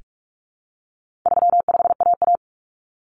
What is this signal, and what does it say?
Morse code, heard as a keyed tone, about 44 wpm, 700 Hz. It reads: 35AA
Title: Common Cut Numbers - 4-Digit 45wpm